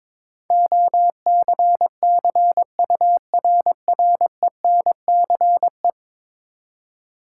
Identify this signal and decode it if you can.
Morse code, heard as a keyed tone, about 22 wpm, 700 Hz. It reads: OCCURRENCE
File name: practice / 22wpm / 10 Letter Words - First 3 Letters Given - 22wpm